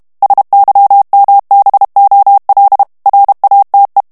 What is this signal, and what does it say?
Morse code, heard as a keyed tone, about 32 wpm, 800 Hz. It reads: SYMBOL RATE
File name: ANGLAIS texte clair 3 vitesse 30 mots minute